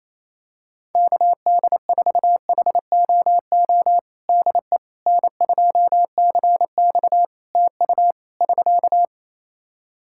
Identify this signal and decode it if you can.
Morse code, heard as a keyed tone, about 28 wpm, 700 Hz. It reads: KD4HOO DE N2CX TU <SK>
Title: Practice QSO - Exchange 4 - Ending - 28wpm